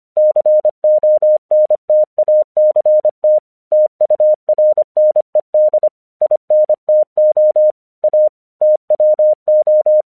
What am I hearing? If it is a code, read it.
Morse code, heard as a keyed tone, about 25 wpm, 610 Hz. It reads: CONTACT TURNED INTO A TWO